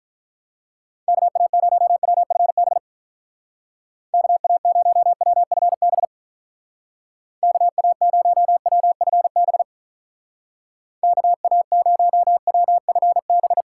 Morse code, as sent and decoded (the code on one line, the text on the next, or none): -.- .- ----- .-- ..-. -... / -.- .- ----- .-- ..-. -... / -.- .- ----- .-- ..-. -... / -.- .- ----- .-- ..-. -...
KA0WFB KA0WFB KA0WFB KA0WFB